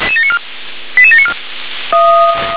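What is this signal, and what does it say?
Five-tone selective call (ZVEI), heard as quick runs of five tones, tones of about 70 ms. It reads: 88794
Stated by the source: Five-tone ZVEI selective call received on SDR.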